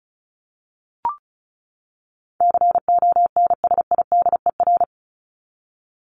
Morse code, heard as a keyed tone, about 34 wpm, 700 Hz. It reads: CONSIDER
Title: Top 501-750 Words 35wpm